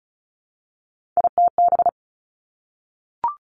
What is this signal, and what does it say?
Morse code, heard as a keyed tone, about 36 wpm, 700 Hz. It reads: ITB
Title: Three Challenging Letter-Number Combinations 35wpm